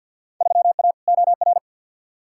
Morse code, heard as a keyed tone, about 53 wpm, 700 Hz. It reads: 3A OR
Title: Field Day Exchanges - 50wpm